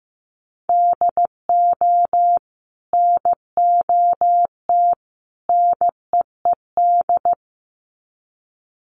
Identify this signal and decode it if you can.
Morse code, heard as a keyed tone, about 15 wpm, 700 Hz. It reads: DO NOT NEED
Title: Sets of 3 Words 15wpm